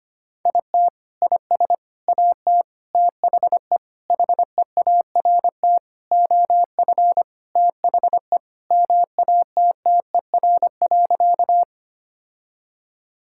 Morse code, heard as a keyed tone, about 25 wpm, 700 Hz. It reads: IT IS AT THE HEART OF THE MATTER.
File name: practice / 25wpm / Sentences with 4-Word Phrases - 25wpm